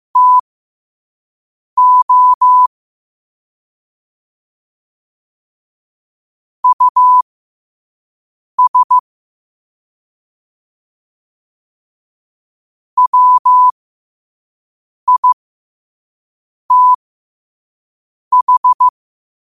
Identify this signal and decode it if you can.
Morse code, heard as a keyed tone, about 15 wpm, spaced as for 5 wpm, 1000 Hz. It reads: TO US WITH
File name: qotdfiles / 5wpm